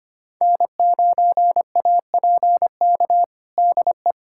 Morse code, heard as a keyed tone, about 25 wpm, 700 Hz. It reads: N9APK DE
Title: USA Callsign DE Callsign - 5 Characters - 25wpm